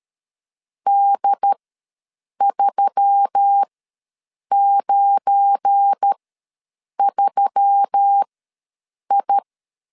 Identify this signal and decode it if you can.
Morse code, heard as a keyed tone, about 13 wpm, spaced as for 7 wpm, 800 Hz. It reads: D393I
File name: LEZIONI / 8 YJ45 / 8 Y ws 12wpm 3409